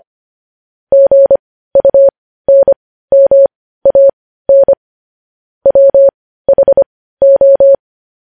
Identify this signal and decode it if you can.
Morse code, heard as a keyed tone, about 25 wpm, spaced as for 15 wpm, 550 Hz. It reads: GUNMAN WHO